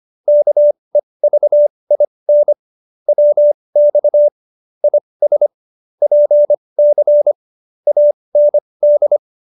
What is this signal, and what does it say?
Morse code, heard as a keyed tone, about 25 wpm, spaced as for 20 wpm, 590 Hz. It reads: KEVIN WX IS PC AND